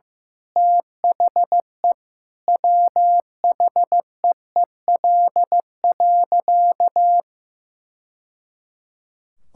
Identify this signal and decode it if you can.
Morse code, heard as a keyed tone, about 15 wpm, 700 Hz. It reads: THE WHEEL.
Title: Sentences from Top 1000 Words - Review Part 5 - 15wpm